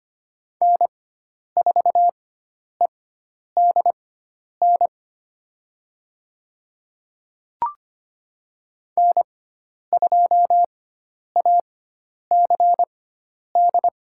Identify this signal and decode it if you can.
Morse code, heard as a keyed tone, about 25 wpm, spaced as for 10 wpm, 700 Hz. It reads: N4EDN N2ACD
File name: Lesson 16g - Letter c - Call Signs 25wpm